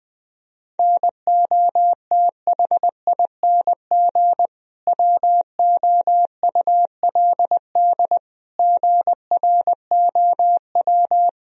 Morse code, heard as a keyed tone, about 20 wpm, 700 Hz. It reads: NOTHING WOULD GROW
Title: Sentences from Top 1000 Words - Review Part 3 - 20wpm